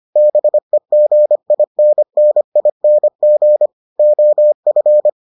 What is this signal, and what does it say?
Morse code, heard as a keyed tone, about 25 wpm, 600 Hz. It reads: BEGINNING OF